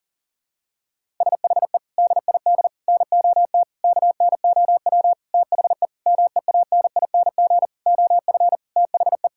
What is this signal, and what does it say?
Morse code, heard as a keyed tone, about 37 wpm, 700 Hz. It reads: SHE DID NOT KNOW THE MEANING OF THE
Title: Sentences from Top 1000 Words - Review Part 6 - 40wpm